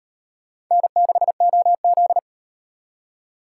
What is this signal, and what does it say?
Morse code, heard as a keyed tone, about 38 wpm, 700 Hz. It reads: N6OZ